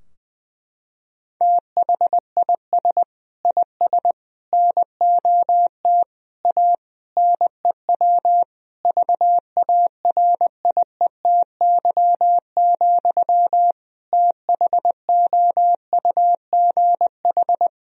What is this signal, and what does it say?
Morse code, heard as a keyed tone, about 20 wpm, 700 Hz. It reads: THIS IS NOT A NEW VARIETY, THOUGH
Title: Sentences from Top 1000 Words - Review Part 4 - 20wpm